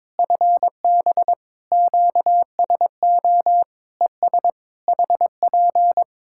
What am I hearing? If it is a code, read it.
Morse code, heard as a keyed tone, about 22 wpm, 700 Hz. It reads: FB QSO ES HP
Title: Practice QSO - Exchange 4 - Ending - 22wpm